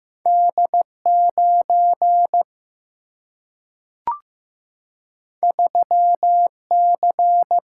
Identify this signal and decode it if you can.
Morse code, heard as a keyed tone, about 15 wpm, 700 Hz. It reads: D 9 3 C
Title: Single-Letter Single-Number 15wpm